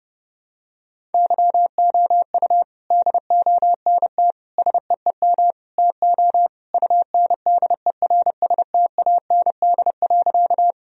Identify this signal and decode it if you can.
Morse code, heard as a keyed tone, about 30 wpm, 700 Hz. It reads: YOU DONT SEEM TO UNDERSTAND.
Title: Sentences from Top 1000 Words - Review Part 1 - 30wpm